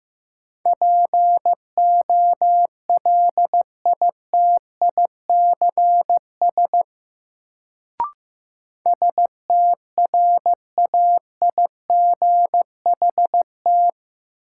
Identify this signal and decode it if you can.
Morse code, heard as a keyed tone, about 15 wpm, 700 Hz. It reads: POLITICS STRAIGHT
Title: Sentences from Top 1000 Words - New Word Drill - 15wpm